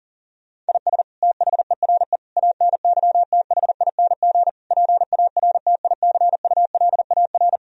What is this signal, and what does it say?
Morse code, heard as a keyed tone, about 37 wpm, 700 Hz. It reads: IS THERE ANYTHING PARTICULAR
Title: Sentences from Top 1000 Words - Review Part 4 - 40wpm